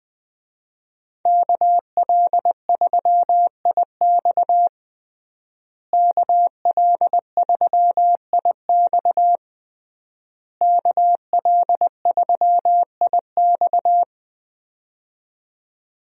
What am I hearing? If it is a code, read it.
Morse code, heard as a keyed tone, about 20 wpm, 700 Hz. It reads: KL3IX KL3IX KL3IX
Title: USA Call Signs - 5 Characters - Repeated 3x - 20wpm